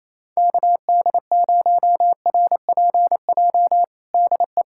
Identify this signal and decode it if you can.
Morse code, heard as a keyed tone, about 28 wpm, 700 Hz. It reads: KD0RPJ DE
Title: USA Callsign DE Callsign - 6 Characters - 28wpm